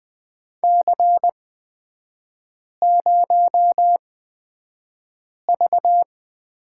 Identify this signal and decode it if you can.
Morse code, heard as a keyed tone, about 20 wpm, spaced as for 5 wpm, 700 Hz. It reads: C0V
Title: 250 Lesson 40e - BK - Three Characters 20wpm